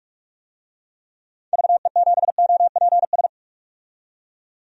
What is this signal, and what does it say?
Morse code, heard as a keyed tone, about 43 wpm, 700 Hz. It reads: VE7OPS
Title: DXCC-100 Call Signs - 6 Characters - 45wpm